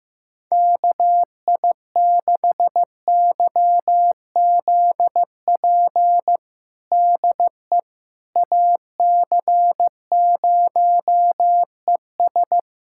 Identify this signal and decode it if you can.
Morse code, heard as a keyed tone, about 15 wpm, 700 Hz. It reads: KI6YZP DE AC0ES
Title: USA Callsign DE Callsign 15wpm